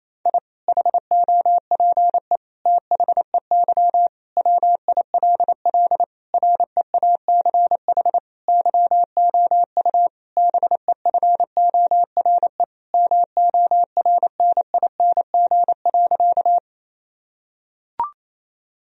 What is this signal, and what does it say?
Morse code, heard as a keyed tone, about 28 wpm, 700 Hz. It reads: I HOPE THEY WILL REACH YOU BEFORE MORNING.
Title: Sentences from Top 1000 Words - Review Part 2 - 28wpm